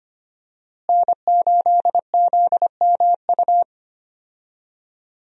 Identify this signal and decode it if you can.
Morse code, heard as a keyed tone, about 25 wpm, 700 Hz. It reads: N8ZMU
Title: USA Call Signs - 5 Characters - Repeated 3x - 25wpm